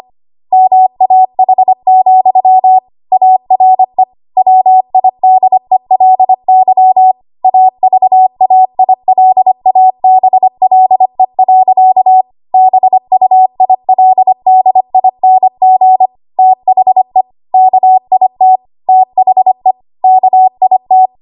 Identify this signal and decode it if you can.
Morse code, heard as a keyed tone, about 25 wpm, 750 Hz. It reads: MAH, ARE WIDELY AVAILABLE. BUILDING THE KIT THE KIT